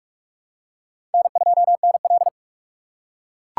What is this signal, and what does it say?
Morse code, heard as a keyed tone, about 41 wpm, 700 Hz. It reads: N2NL